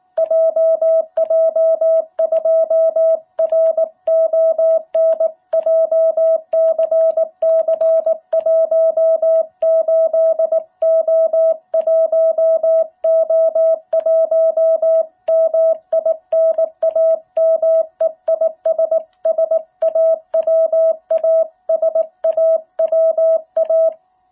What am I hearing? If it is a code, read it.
Morse code, heard as a keyed tone, about 19 wpm, 660 Hz. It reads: JJ2RON JCC18O1O1 MI NAME IS SAWA SAWA
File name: jh8cbh / 2021cw